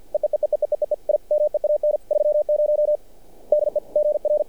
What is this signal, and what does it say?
Morse code, heard as a keyed tone, about 48 wpm, 600 Hz. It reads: EEEEEEEEE I MEAN 30 DE GR
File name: archived / CRSnet